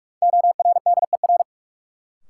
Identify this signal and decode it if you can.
Morse code, heard as a keyed tone, about 46 wpm, 700 Hz. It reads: ORDER
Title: Sets of 2 Words - New Word Drill - 45wpm